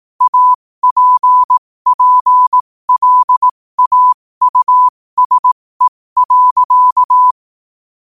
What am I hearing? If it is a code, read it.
Morse code, heard as a keyed tone, about 18 wpm, spaced as for 15 wpm, 1000 Hz. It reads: APPLAUSE.
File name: qotdfiles / 15wpm